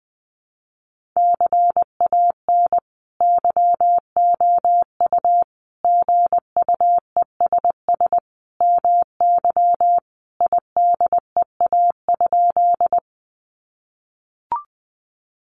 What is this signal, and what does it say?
Morse code, heard as a keyed tone, about 20 wpm, 700 Hz. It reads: CAN YOU GUESS MY IDEA?
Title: Sentences from Top 1500 Words 20wpm